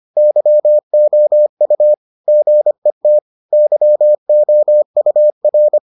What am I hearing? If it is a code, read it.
Morse code, heard as a keyed tone, about 25 wpm, 600 Hz. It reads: YOU GET YOUR